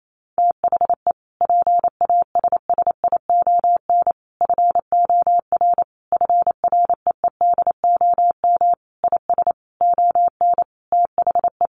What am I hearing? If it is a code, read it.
Morse code, heard as a keyed tone, about 28 wpm, 700 Hz. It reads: THE PASSION FOR FREEDOM IS ON THE
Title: Sentences from Top 1300 Words 28wpm